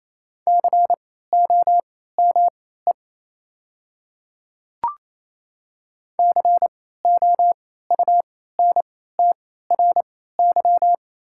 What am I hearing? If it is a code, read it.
Morse code, heard as a keyed tone, about 28 wpm, spaced as for 16 wpm, 700 Hz. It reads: COME COUNTRY
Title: 3x Character Spacing - Top 300 Words - 28wpm